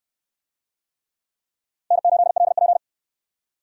Morse code, heard as a keyed tone, about 68 wpm, 700 Hz. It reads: N8LP